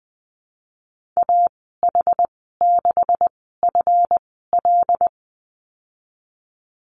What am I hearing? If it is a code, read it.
Morse code, heard as a keyed tone, about 19 wpm, spaced as for 14 wpm, 700 Hz. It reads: AH6FL
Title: 147 Lesson 24g - Letter y - Call Signs 20wpm